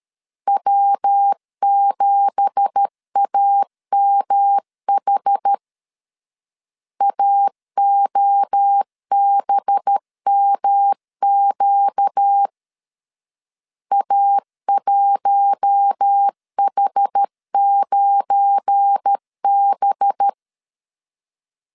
Morse code, heard as a keyed tone, about 13 wpm, 800 Hz. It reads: W7AMH AOBMQ A1H9B
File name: CW / DET / 1 9 1500C S C 12wpm 3811